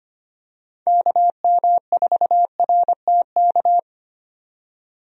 Morse code, heard as a keyed tone, about 25 wpm, 700 Hz. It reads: KM4RTK
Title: USA Call Signs - 6 Characters - 25wpm